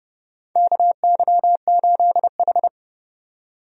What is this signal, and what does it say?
Morse code, heard as a keyed tone, about 29 wpm, 700 Hz. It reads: KY8H
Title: USA Call Signs - 4 Characters - 30wpm